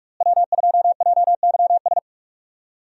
Morse code, heard as a keyed tone, about 43 wpm, 700 Hz. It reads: W2JYS
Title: USA Call Signs - 5 Characters - Repeated 2x - 45wpm